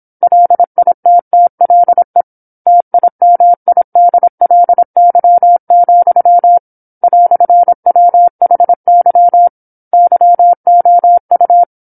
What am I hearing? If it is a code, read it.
Morse code, heard as a keyed tone, about 26 wpm, 700 Hz. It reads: LITTLE TIMIDLY, "WHY YOU
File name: 08-Chapter-08 26 WPM